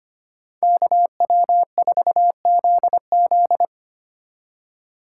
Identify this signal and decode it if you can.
Morse code, heard as a keyed tone, about 31 wpm, 700 Hz. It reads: KW4ZZ
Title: Speed Racing - USA 2x2 Call Signs 25wpm